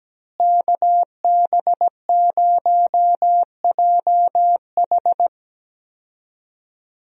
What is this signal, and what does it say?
Morse code, heard as a keyed tone, about 17 wpm, 700 Hz. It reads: KB0JH
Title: USA Call Signs - 5 Characters - Repeated 3x - 17wpm